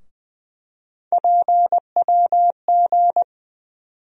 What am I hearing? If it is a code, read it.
Morse code, heard as a keyed tone, about 20 wpm, 700 Hz. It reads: PWG
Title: USA Call Sign Postfixes - 3 Letters 20wpm